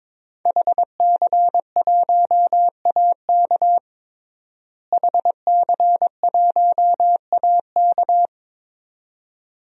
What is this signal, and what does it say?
Morse code, heard as a keyed tone, about 21 wpm, 700 Hz. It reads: HC1AK HC1AK
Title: DXCC-100 Call Signs - 5 Characters - Repeated 2x - 22wpm